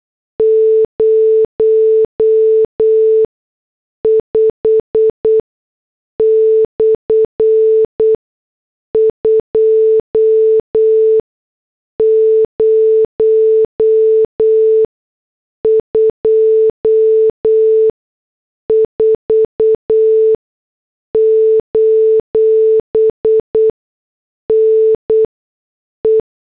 Morse code, heard as a keyed tone, about 8 wpm, 440 Hz. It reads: 0 5 / 2 0 2 4 : N E
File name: morse / headlines